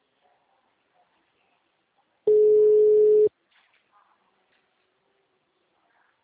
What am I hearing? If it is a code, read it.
Ringback tone (Europe, 425 Hz).
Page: Araras (FGTS)